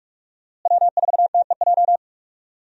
Morse code, heard as a keyed tone, about 46 wpm, 700 Hz. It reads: W4TEJ
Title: USA Call Signs - 5 Characters - Repeated 3x - 45wpm